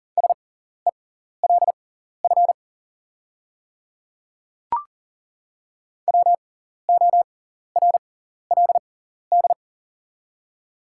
Morse code, heard as a keyed tone, about 40 wpm, spaced as for 14 wpm, 700 Hz. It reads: SELF WORLD
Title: 6x Character Spacing - Top 500 Words - 40wpm